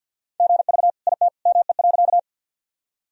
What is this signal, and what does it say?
Morse code, heard as a keyed tone, about 50 wpm, 700 Hz. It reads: GV IT ME.
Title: Sentences from Top 200 Words and Abbreviations 50wpm